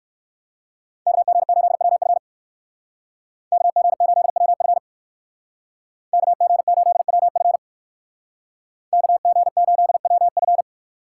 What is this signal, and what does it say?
Morse code, heard as a keyed tone, about 56 wpm, 700 Hz. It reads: KG8WF KG8WF KG8WF KG8WF
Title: Speed Racing - USA 2x2 Call Signs 45wpm